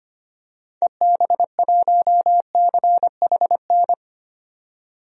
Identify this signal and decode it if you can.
Morse code, heard as a keyed tone, about 25 wpm, 700 Hz. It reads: EB1CHN
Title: DXCC-100 Call Signs - 6 Characters - 25wpm